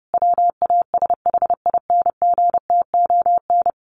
Morse code, heard as a keyed tone, about 30 wpm, 700 Hz. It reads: WASHINGTON
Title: Sets of 5 Words Encore2 30wpm